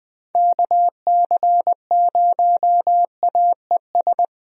Morse code, heard as a keyed tone, about 20 wpm, 700 Hz. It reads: KC0AES
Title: USA Call Signs - 6 Characters - Repeated 3x - 20wpm